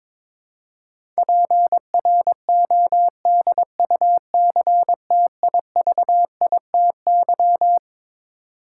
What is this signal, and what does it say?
Morse code, heard as a keyed tone, about 22 wpm, 700 Hz. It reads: PRODUCTIVITY
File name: Twelve Letter Words - 22wpm